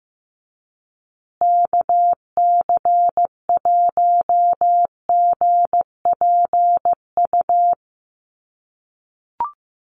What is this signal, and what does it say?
Morse code, heard as a keyed tone, about 15 wpm, 700 Hz. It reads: KC1GPU
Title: USA Call Signs Repeated 2x 15wpm